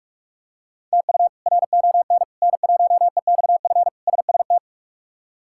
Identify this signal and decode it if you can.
Morse code, heard as a keyed tone, about 43 wpm, 700 Hz. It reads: TU RON N1EXF SST
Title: K1USN SST QSO Practice - 45wpm